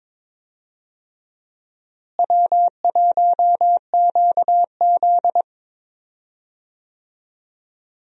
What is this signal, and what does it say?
Morse code, heard as a keyed tone, about 22 wpm, 700 Hz. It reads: W1QZ
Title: USA Call Signs - 4 Characters - Repeated 2x - 22wpm